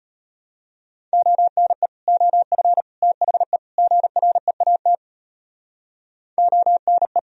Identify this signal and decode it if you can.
Morse code, heard as a keyed tone, about 31 wpm, 700 Hz. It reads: ONE OF THE GREAT ONE
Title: Speed-Racing - Sets of 4 Words 25wpm